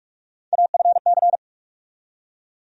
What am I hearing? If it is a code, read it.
Morse code, heard as a keyed tone, about 44 wpm, 700 Hz. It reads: AFC
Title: USA Call Sign Postfixes - 3 Letters 45wpm